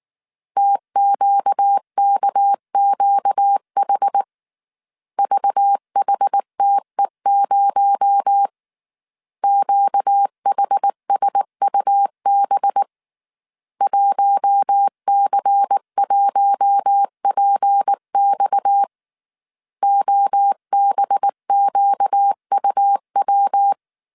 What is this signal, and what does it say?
Morse code, heard as a keyed tone, about 19 wpm, 800 Hz. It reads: TQKQH VHTE0 QHSUB 1C1PX OBQUW
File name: CW / DET / 1 6 1500C S A 18wpm 2255